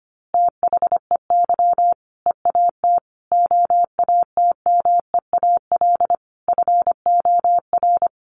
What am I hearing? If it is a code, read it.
Morse code, heard as a keyed tone, about 25 wpm, 700 Hz. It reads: THEY EAT OATMEAL FOR
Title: Easy Phrases 25wpm